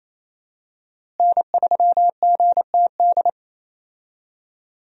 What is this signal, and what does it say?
Morse code, heard as a keyed tone, about 28 wpm, 700 Hz. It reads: N3GTD
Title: USA Call Signs - 5 Characters - Repeated 2x - 28wpm